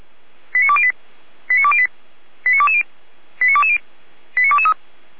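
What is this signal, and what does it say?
Five-tone selective call (ZVEI), heard as quick runs of five tones, tones of about 70 ms. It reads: 89208 89208 89220 89220 89223
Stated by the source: ZVEI five-tone selective call received on SDR.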